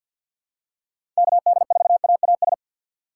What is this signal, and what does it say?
Morse code, heard as a keyed tone, about 46 wpm, 700 Hz. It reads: KD4AAS
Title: USA Call Signs - 6 Characters - 3x Character Spacing - 50wpm